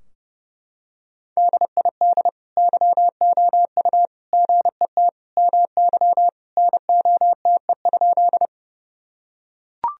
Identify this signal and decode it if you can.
Morse code, heard as a keyed tone, about 30 wpm, 700 Hz. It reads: DID YOU GET MY NOTE?
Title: Sentences from Top 1000 Words - Review Part 2 - 30wpm